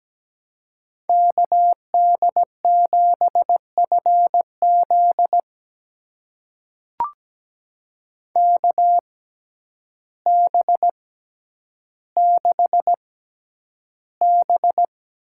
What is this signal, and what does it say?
Morse code, heard as a keyed tone, about 17 wpm, spaced as for 9 wpm, 700 Hz. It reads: KD7FZ KB6B
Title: USA Call Signs - 5 Characters - 6x Character Spacing - 17wpm